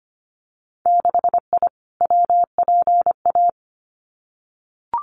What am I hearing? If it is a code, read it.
Morse code, heard as a keyed tone, about 25 wpm, 700 Hz. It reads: 6I WPA
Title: Winter Field Day Exchanges Encore 25wpm